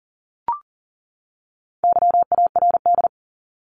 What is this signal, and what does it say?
Morse code, heard as a keyed tone, about 40 wpm, 700 Hz. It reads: YARD
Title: Top 1751-2000 Words 40wpm